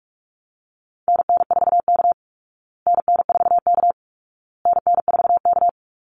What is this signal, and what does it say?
Morse code, heard as a keyed tone, about 44 wpm, 700 Hz. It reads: NN4K NN4K NN4K
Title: Call Signs Repeated 2x - 45wpm